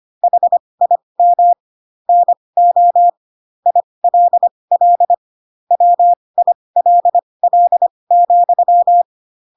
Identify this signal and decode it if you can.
Morse code, heard as a keyed tone, about 25 wpm, spaced as for 20 wpm, 700 Hz. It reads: HIM NO ILL WILL,
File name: Pinocchio / Pinocchio ch2 15 25 ik1hge